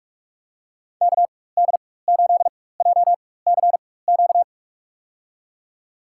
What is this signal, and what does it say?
Morse code, heard as a keyed tone, about 44 wpm, spaced as for 21 wpm, 700 Hz. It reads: KD8JCQ
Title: USA Call Signs - 4x Character Spacing - 45wpm